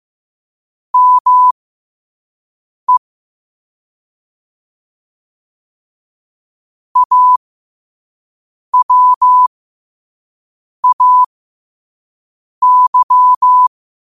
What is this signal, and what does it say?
Morse code, heard as a keyed tone, about 15 wpm, spaced as for 5 wpm, 1000 Hz. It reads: ME AWAY
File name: 5wpm